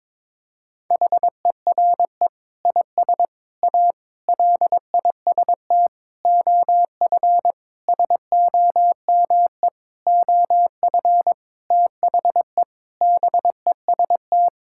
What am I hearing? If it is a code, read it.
Morse code, heard as a keyed tone, about 22 wpm, 700 Hz. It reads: HERE IS A LIST OF SOME OF THE BEST
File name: practice / 22wpm / Sentences with 8-Word Phrases - 22wpm